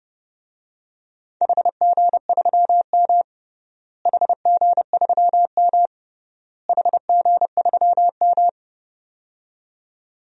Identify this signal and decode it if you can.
Morse code, heard as a keyed tone, about 29 wpm, 700 Hz. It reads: HG3M HG3M HG3M
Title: DXCC-100 Call Signs - 4 Characters - Repeated 3x - 30wpm